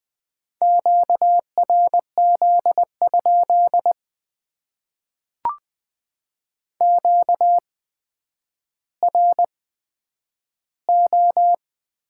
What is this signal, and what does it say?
Morse code, heard as a keyed tone, about 20 wpm, spaced as for 10 wpm, 700 Hz. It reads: QRZ? QRO
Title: Top 90 QSO Elements - 8x Character Spacing - 20wpm